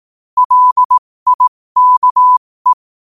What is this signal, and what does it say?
Morse code, heard as a keyed tone, about 18 wpm, spaced as for 15 wpm, 1000 Hz. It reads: LIKE
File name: qotdfiles / 15wpm